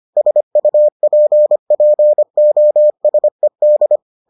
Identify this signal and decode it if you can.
Morse code, heard as a keyed tone, about 25 wpm, 600 Hz. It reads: SUPPOSED